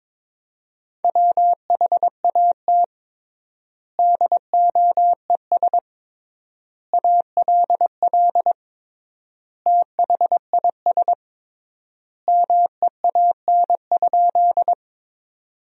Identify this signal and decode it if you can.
Morse code, heard as a keyed tone, about 22 wpm, 700 Hz. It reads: WHAT DOES ALL THIS MEAN?
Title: 3x Word-Spacing - Sentences from Top 200 Words - 22wpm